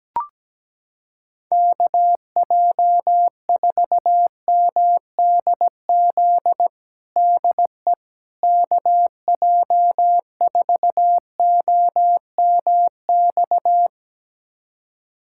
Morse code, heard as a keyed tone, about 17 wpm, 700 Hz. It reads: KJ4MDZ DE KJ4OMX
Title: USA Callsign DE Callsign - 6 Characters - 17wpm